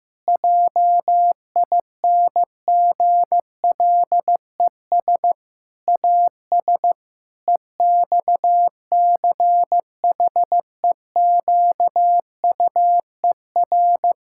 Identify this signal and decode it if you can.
Morse code, heard as a keyed tone, about 15 wpm, 700 Hz. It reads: JINGLES AS EXCHEQUER
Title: Pangram Challenge v2 - 15wpm